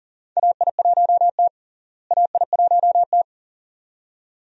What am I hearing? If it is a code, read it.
Morse code, heard as a keyed tone, about 40 wpm, 700 Hz. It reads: AI1T AI1T
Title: USA Call Signs - 4 Characters - Repeated 2x - 40wpm